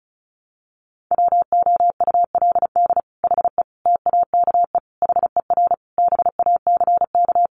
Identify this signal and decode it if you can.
Morse code, heard as a keyed tone, about 34 wpm, 700 Hz. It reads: WOULD HE TAKE HER BACK
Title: Sentences from Top 1500 Words 35wpm